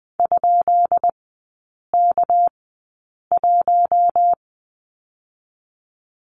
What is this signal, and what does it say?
Morse code, heard as a keyed tone, about 20 wpm, spaced as for 8 wpm, 700 Hz. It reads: ?K1
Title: 198 Lesson 32e - Letter k - Three Characters 20wpm